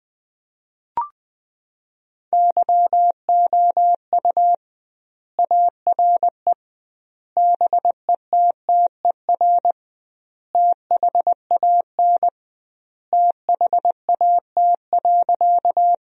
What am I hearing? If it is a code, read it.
Morse code, heard as a keyed tone, about 21 wpm, 700 Hz. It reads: YOU ARE BETTER THAN THAT.
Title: 2x Word-Spacing - Sentences from Top 400 Words - 20wpm